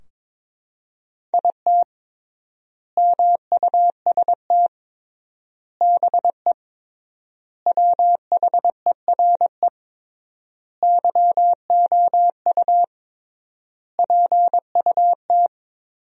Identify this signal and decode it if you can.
Morse code, heard as a keyed tone, about 21 wpm, 700 Hz. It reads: IT MUST BE WHERE YOU PUT
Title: 3x Word-Spacing - Sentences from Top 200 Words - 22wpm